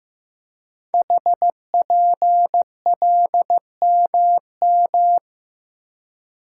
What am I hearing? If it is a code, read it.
Morse code, heard as a keyed tone, about 15 wpm, 700 Hz. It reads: HPLMM
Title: Code Groups - 5 Letters - 15wpm